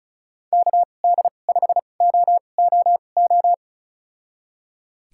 Morse code, heard as a keyed tone, about 34 wpm, spaced as for 25 wpm, 700 Hz. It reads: KD5OOO
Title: USA Call Signs - 6 Characters - 2x Character Spacing - 35wpm